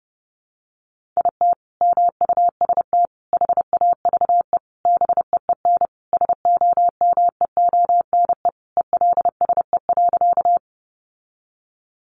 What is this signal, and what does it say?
Morse code, heard as a keyed tone, about 30 wpm, 700 Hz. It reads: IT MUST HAVE BEEN SOMEONE ELSE.
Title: Sentences from Top 800 Words 30wpm